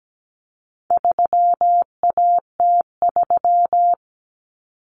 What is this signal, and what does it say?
Morse code, heard as a keyed tone, about 17 wpm, 700 Hz. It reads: 3AT3
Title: Common Cut Numbers - 4-Digit 17wpm